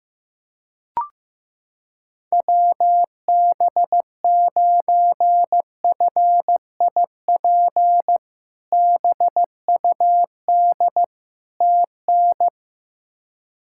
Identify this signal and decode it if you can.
Morse code, heard as a keyed tone, about 16 wpm, 700 Hz. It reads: WB9FIP BUD TN
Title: K1USN SST QSO Practice - 15wpm